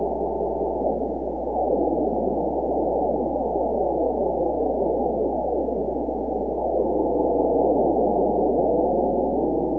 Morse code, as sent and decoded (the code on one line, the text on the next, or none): none